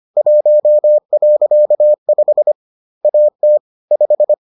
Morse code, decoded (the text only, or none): 1.5 AT 5